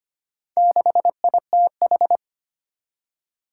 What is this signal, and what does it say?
Morse code, heard as a keyed tone, about 25 wpm, 700 Hz. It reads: 6ITH
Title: Four Challenging Letter-Number Combinations 25wpm